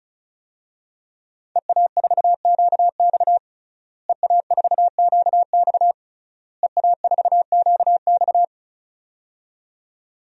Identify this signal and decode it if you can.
Morse code, heard as a keyed tone, about 34 wpm, 700 Hz. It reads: EA4QX EA4QX EA4QX
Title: DXCC-100 Call Signs - 5 Characters - Repeated 3x - 35wpm